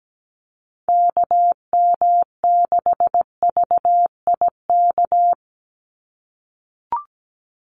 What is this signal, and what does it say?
Morse code, heard as a keyed tone, about 17 wpm, 700 Hz. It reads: KM6VIK
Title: USA Call Signs 17wpm